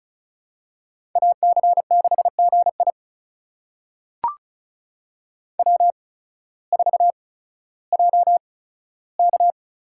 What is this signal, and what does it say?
Morse code, heard as a keyed tone, about 34 wpm, spaced as for 16 wpm, 700 Hz. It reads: AC6GI W4JK
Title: USA Call Signs - 5 Characters - 8x Character Spacing - 35wpm